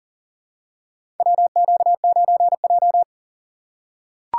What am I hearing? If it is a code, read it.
Morse code, heard as a keyed tone, about 39 wpm, 700 Hz. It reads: WQ9J
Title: USA Call Signs - 4 Characters - 6x Character Spacing - 40wpm